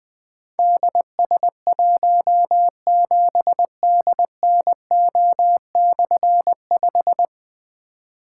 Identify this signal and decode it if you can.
Morse code, heard as a keyed tone, about 20 wpm, 700 Hz. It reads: DS17DNO/5
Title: Difficult Call Signs 20wpm